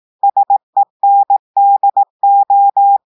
Morse code, heard as a keyed tone, about 18 wpm, 800 Hz. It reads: SENDO
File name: audios / portugais / portugais texte en clair 1 vitesse 18 mots minutes